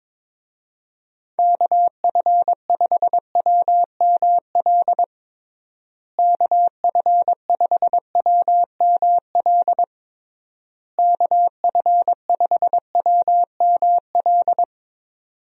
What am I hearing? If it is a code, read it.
Morse code, heard as a keyed tone, about 22 wpm, 700 Hz. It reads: KF5WML KF5WML KF5WML
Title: USA Call Signs - 6 Characters - Repeated 3x - 22wpm